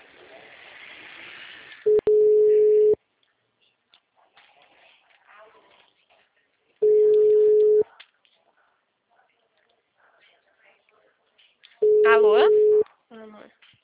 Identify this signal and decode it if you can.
Ringback tone (Europe, 425 Hz), with speech.
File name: out- -2031- - - .55508